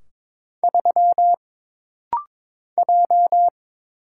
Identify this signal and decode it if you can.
Morse code, heard as a keyed tone, about 22 wpm, 700 Hz. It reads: 3 J
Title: Single Letter-Number Alt - Rapid-Fire 22wpm